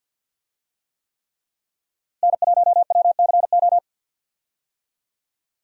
Morse code, heard as a keyed tone, about 46 wpm, 700 Hz. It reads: N1WCO
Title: USA Call Signs - 5 Characters - Repeated 3x - 50wpm